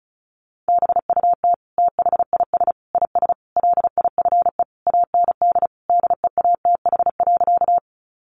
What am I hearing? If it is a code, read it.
Morse code, heard as a keyed tone, about 34 wpm, 700 Hz. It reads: BUT THIS IS LIFE AND DEATH.
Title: Sentences from Top 1100 Words 35wpm